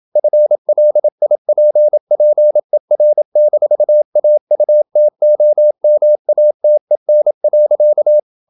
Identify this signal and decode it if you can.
Morse code, heard as a keyed tone, about 27 wpm, 600 Hz. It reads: FLIPPER-AUTOMATEN.
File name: Ferie QRQ 27wpm